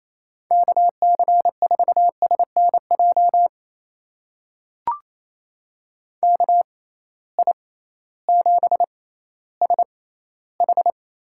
Morse code, heard as a keyed tone, about 28 wpm, spaced as for 15 wpm, 700 Hz. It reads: KC4SNJ KI7SH
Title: USA Call Signs - 6 Characters - 6x Character Spacing - 28wpm